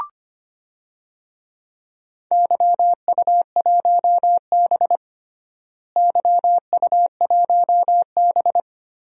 Morse code, heard as a keyed tone, about 25 wpm, 700 Hz. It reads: YU1B YU1B
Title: DXCC-100 Call Signs - 4 Characters - Repeated 3x - 25wpm